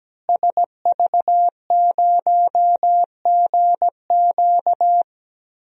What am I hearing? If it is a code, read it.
Morse code, heard as a keyed tone, about 17 wpm, 700 Hz. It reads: SV0GQ
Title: DXCC-100 Call Signs - 5 Characters - Repeated 3x - 17wpm